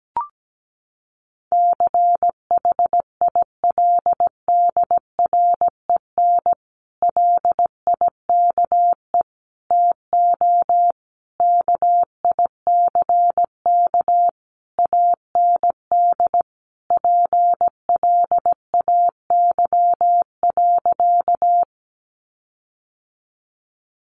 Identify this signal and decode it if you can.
Morse code, heard as a keyed tone, about 17 wpm, 700 Hz. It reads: CHILDREN LIKE TO KICK AND PLAY.
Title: Sentences from Top 1300 Words 17wpm